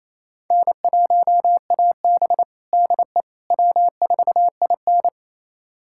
Morse code, heard as a keyed tone, about 28 wpm, 700 Hz. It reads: N1AB DE W4IN
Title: USA Callsign DE Callsign - 4 Characters - 28wpm